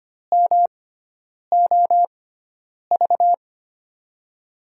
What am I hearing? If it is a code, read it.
Morse code, heard as a keyed tone, about 25 wpm, spaced as for 9 wpm, 700 Hz. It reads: MOV
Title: USA Call Signs - 6 Characters - 6x Character Spacing - 25wpm